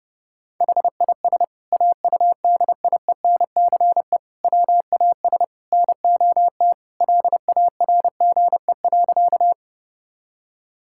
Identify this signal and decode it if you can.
Morse code, heard as a keyed tone, about 32 wpm, 700 Hz. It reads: HIS AUDIENCE WAS NOT LARGE.
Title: Sentences from Top 1000 Words - Review Part 5 - 30wpm